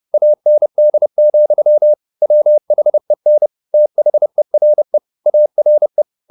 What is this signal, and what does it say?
Morse code, heard as a keyed tone, about 30 wpm, 600 Hz. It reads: AND, WHEN THERE ARE
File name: the old man and the sea / the old man and the sea 1 30wpm